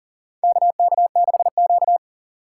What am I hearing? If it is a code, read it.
Morse code, heard as a keyed tone, about 40 wpm, 700 Hz. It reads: KK6Q
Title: USA Call Signs - 4 Characters - 6x Character Spacing - 40wpm